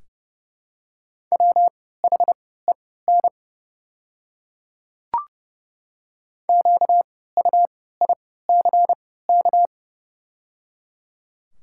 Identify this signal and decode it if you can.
Morse code, heard as a keyed tone, about 30 wpm, spaced as for 17 wpm, 700 Hz. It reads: WHEN QUICK
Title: 3x Character Spacing - Top 500 Words - 30wpm